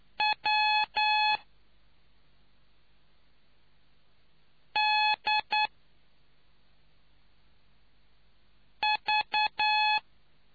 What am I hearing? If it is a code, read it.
Morse code, heard as a keyed tone, about 9 wpm, spaced as for 2 wpm, 850 Hz. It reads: WDV